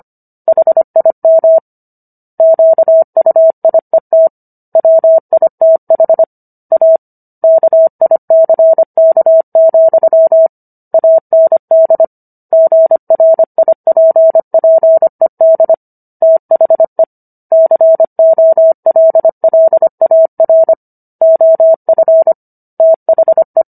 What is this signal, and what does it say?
Morse code, heard as a keyed tone, about 25 wpm, 650 Hz. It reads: HIM QUIET WITH A KICK, AND GRIPPED THE COLLAR OF THE